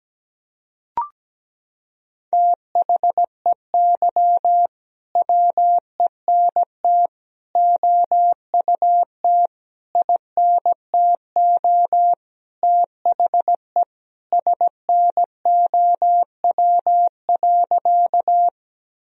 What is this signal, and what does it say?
Morse code, heard as a keyed tone, about 17 wpm, 700 Hz. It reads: THEY WENT OUT INTO THE SNOW.
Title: Sentences from Top 1000 Words - Review Part 3 - 17wpm